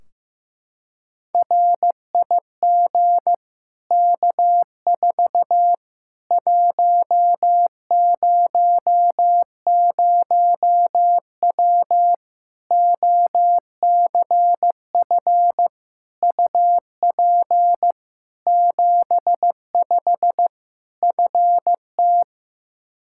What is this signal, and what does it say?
Morse code, heard as a keyed tone, about 15 wpm, 700 Hz. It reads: RIG K4 100W OCF UP 75 FT
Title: Practice QSO - Exchange 2 - Info - 15wpm